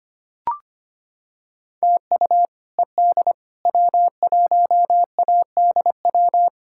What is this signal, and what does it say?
Morse code, heard as a keyed tone, about 25 wpm, 700 Hz. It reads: TU ED W1ADW
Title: K1USN SST QSO Practice - 25wpm